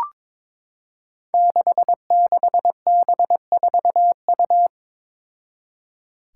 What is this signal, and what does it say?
Morse code, heard as a keyed tone, about 22 wpm, 700 Hz. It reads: 66B4U
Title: Five Challenging Letter-Number Combinations 22wpm